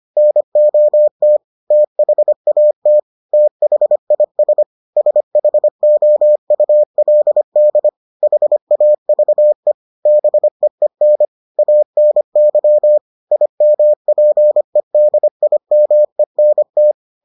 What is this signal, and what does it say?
Morse code, heard as a keyed tone, about 25 wpm, 600 Hz. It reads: NOT THAT THIS SHOULD HAVE BEEN ANY IMPEDIMENT